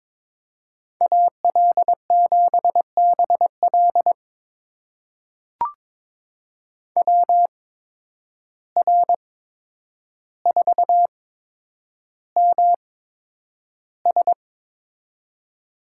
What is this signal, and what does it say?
Morse code, heard as a keyed tone, about 22 wpm, spaced as for 9 wpm, 700 Hz. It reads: AL7BL WR4MS
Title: USA Call Signs - 5 Characters - 8x Character Spacing - 22wpm